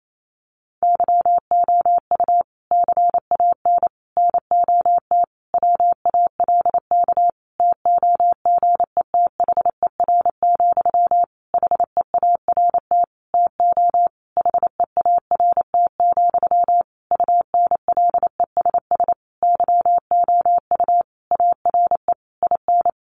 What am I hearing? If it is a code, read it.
Morse code, heard as a keyed tone, about 28 wpm, 700 Hz. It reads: YOU CAN NOT WALK TOGETHER, HEART TO HEART, UNLESS YOU ARE IN
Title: Sentences from Top 1500 Words 28wpm